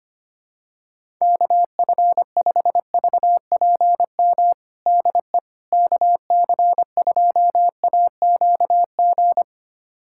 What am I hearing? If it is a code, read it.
Morse code, heard as a keyed tone, about 25 wpm, 700 Hz. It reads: KF5VPM DE KC2AQG
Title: USA Callsign DE Callsign - 6 Characters - 25wpm